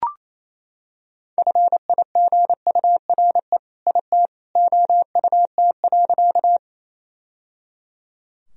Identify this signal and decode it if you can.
Morse code, heard as a keyed tone, about 28 wpm, 700 Hz. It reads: FIGURE IT OUT.
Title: Sentences from Top 1000 Words - Review Part 2 - 28wpm